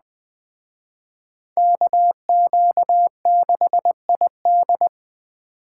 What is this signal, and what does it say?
Morse code, heard as a keyed tone, about 20 wpm, 700 Hz. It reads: KQ6ID
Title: Speed Racing - USA 2x2 Call Signs 17wpm